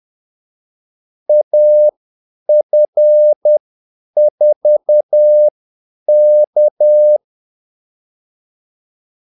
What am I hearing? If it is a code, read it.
Morse code, heard as a keyed tone, about 10 wpm, spaced as for 8 wpm, 600 Hz. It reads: AF4K
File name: Callsigns-10-8wpm